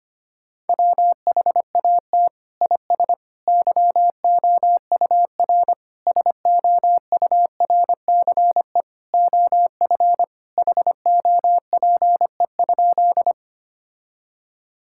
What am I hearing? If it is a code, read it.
Morse code, heard as a keyed tone, about 25 wpm, 700 Hz. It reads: WHAT IS YOUR SOURCE OF HOPE?
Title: Sentences from Top 1000 Words - Review Part 5 - 25wpm